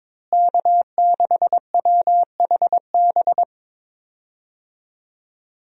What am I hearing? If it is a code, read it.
Morse code, heard as a keyed tone, about 22 wpm, 700 Hz. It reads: K6WHB
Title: USA Call Signs - 5 Characters - Repeated 2x - 22wpm